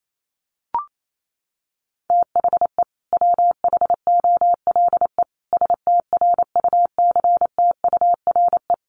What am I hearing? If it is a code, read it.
Morse code, heard as a keyed tone, about 28 wpm, 700 Hz. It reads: THE WHOLE STRUCTURE
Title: Sentences from Top 1500 Words 28wpm